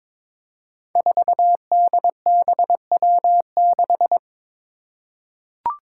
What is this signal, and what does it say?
Morse code, heard as a keyed tone, about 21 wpm, 700 Hz. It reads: 4DBW6
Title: Five Challenging Letter-Number Combinations 22wpm